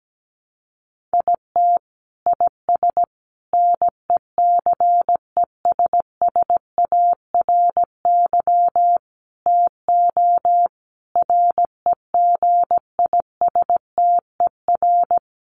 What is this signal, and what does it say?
Morse code, heard as a keyed tone, about 17 wpm, 700 Hz. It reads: IT IS NECESSARY TO REGISTER
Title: Sentences from Top 1200 Words 17wpm